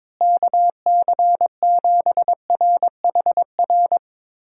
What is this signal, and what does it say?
Morse code, heard as a keyed tone, about 22 wpm, 700 Hz. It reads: KC7RHR
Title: USA Call Signs - 6 Characters - Repeated 3x - 22wpm